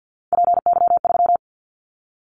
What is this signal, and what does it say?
Morse code, heard as a keyed tone, about 50 wpm, 700 Hz. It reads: PY3
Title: Callsign Prefixes for Top 50 DXCC - 50wpm